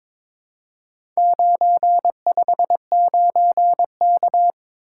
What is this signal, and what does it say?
Morse code, heard as a keyed tone, about 22 wpm, 700 Hz. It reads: 959K
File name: practice / 22wpm / Four Letter-Number Combinations - 22wpm